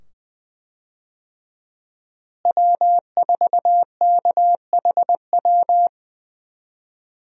Morse code, heard as a keyed tone, about 20 wpm, 700 Hz. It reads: W4KHW
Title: USA Call Signs - 5 Characters - Repeated 2x - 20wpm